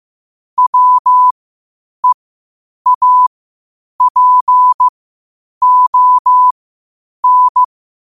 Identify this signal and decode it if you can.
Morse code, heard as a keyed tone, about 15 wpm, spaced as for 8 wpm, 1000 Hz. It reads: WEAPON
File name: qotdfiles / 8wpm